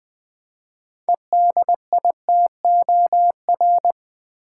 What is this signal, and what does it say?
Morse code, heard as a keyed tone, about 20 wpm, 700 Hz. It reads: EDITOR
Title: Build A Word - 3 Letters At A Time - 9-Letter Words - 20wpm